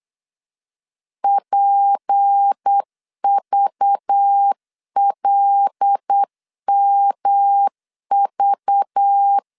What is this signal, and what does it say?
Morse code, heard as a keyed tone, about 8 wpm, 800 Hz. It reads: PVLMV
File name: LEZIONI / 6 PKX0 / 6 P 8wpm 3025